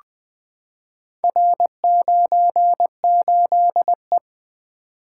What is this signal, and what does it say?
Morse code, heard as a keyed tone, about 20 wpm, 700 Hz. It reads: R98E
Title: Four Letter-Number Combinations - 20wpm